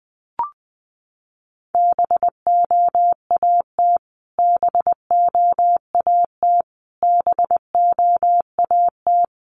Morse code, heard as a keyed tone, about 20 wpm, 700 Hz. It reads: BOAT BOAT BOAT
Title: Top 500 Words Repeated 20wpm